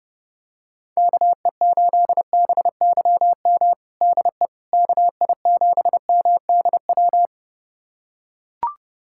Morse code, heard as a keyed tone, about 30 wpm, 700 Hz. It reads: KE8BYM DE KI7MDW
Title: USA Callsign DE Callsign - 6 Characters - 30wpm